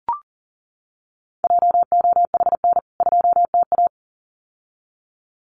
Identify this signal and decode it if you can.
Morse code, heard as a keyed tone, about 40 wpm, 700 Hz. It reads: JOHN 2TA
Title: CWT Exchange 40wpm